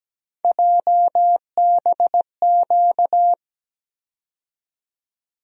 Morse code, heard as a keyed tone, about 17 wpm, 700 Hz. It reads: JBQ
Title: USA Call Sign Postfixes - 3 Letters 17wpm